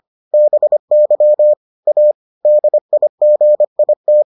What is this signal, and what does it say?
Morse code, heard as a keyed tone, about 25 wpm, 600 Hz. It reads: BY A DIGIT